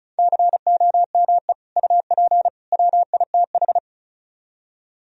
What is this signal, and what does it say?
Morse code, heard as a keyed tone, about 45 wpm, 700 Hz. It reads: COME UP WITH